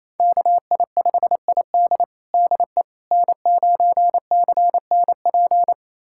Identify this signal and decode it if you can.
Morse code, heard as a keyed tone, about 28 wpm, 700 Hz. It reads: KI5ID DE N9CNP
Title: USA Callsign DE Callsign - 5 Characters - 28wpm